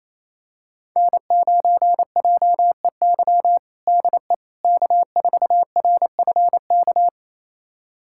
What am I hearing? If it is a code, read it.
Morse code, heard as a keyed tone, about 28 wpm, 700 Hz. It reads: N9JEY DE K4RFK
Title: USA Callsign DE Callsign - 5 Characters - 28wpm